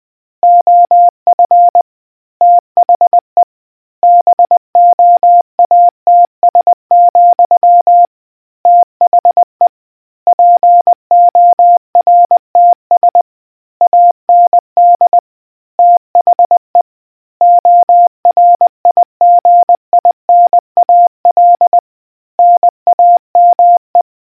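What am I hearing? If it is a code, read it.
Morse code, heard as a keyed tone, about 20 wpm, 700 Hz. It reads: OF THE BOATS, THE PORTS AND THE ORIGINAL NAME
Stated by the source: Morse quiz question for naval history buffs from the RNZN Communicators Association.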